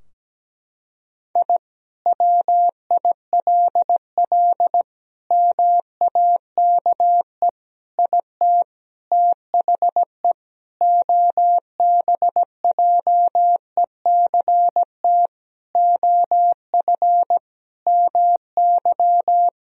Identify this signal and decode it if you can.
Morse code, heard as a keyed tone, about 17 wpm, 700 Hz. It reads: I WILL MAKE IT THE OBJECT OF MY
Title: Sentences from Top 1000 Words - Review Part 3 - 17wpm